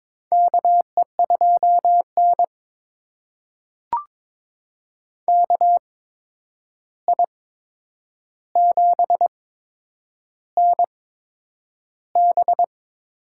Morse code, heard as a keyed tone, about 22 wpm, spaced as for 9 wpm, 700 Hz. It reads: KE2N KI7NB
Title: USA Call Signs - 8x Character Spacing - 22wpm